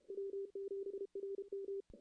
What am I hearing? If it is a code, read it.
Morse code, heard as a keyed tone, about 31 wpm, 400 Hz. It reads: W7RM
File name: CW / CW-SS-k5tr- -1700